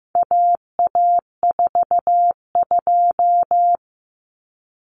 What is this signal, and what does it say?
Morse code, heard as a keyed tone, about 15 wpm, 700 Hz. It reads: AA42
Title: Common Cut Numbers - 4-Digit 15wpm